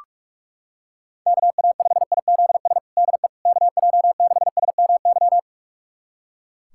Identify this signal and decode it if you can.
Morse code, heard as a keyed tone, about 42 wpm, 700 Hz. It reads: KA5IZS DE KJ6SMY
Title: USA Callsign DE Callsign - 6 Characters - 45wpm